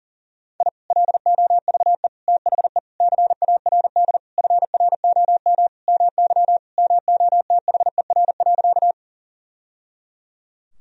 Morse code, heard as a keyed tone, about 40 wpm, 700 Hz. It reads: I LOVE THE CARD FROM MY MOTHER.
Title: Sentences from Top 1000 Words - Review Part 4 - 40wpm